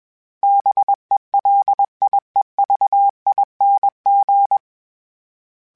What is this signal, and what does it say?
Morse code, heard as a keyed tone, about 21 wpm, 800 Hz. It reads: BELIEVING
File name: Special INGs / Special INGs-mp3s / Sngl Ing @20